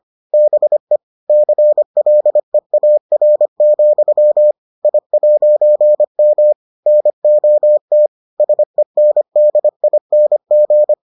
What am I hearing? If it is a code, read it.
Morse code, heard as a keyed tone, about 25 wpm, 600 Hz. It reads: BE CLEAR, I'M NOT SENDING